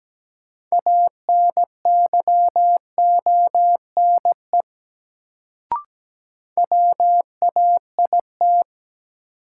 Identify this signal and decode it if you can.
Morse code, heard as a keyed tone, about 17 wpm, 700 Hz. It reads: ANYONE WAIT
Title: Sets of 5 Words Encore - New Word Drill - 17wpm